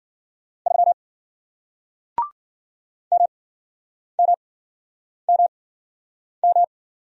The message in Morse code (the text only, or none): 3 MMMM